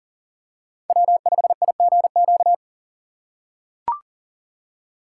Morse code, heard as a keyed tone, about 40 wpm, 700 Hz. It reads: W5IGQ
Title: USA Call Signs - 2x Character Spacing - 40wpm